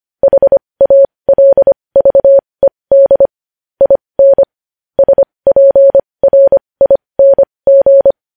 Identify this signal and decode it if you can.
Morse code, heard as a keyed tone, about 25 wpm, spaced as for 20 wpm, 550 Hz. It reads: HALVED IN SPRING
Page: News Headlines in Morse Code at 20 WPM